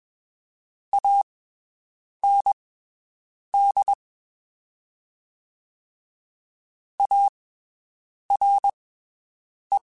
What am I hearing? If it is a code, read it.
Morse code, heard as a keyed tone, about 21 wpm, spaced as for 7 wpm, 800 Hz. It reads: AND ARE
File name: (spaced) / QSOs Spaced-mp3s / WD0BGP 1 @20 sp